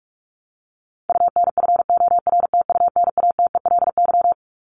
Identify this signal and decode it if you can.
Morse code, heard as a keyed tone, about 44 wpm, 700 Hz. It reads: UNFORTUNATELY
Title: LY Suffix - Top 100 Words with Base Words 45wpm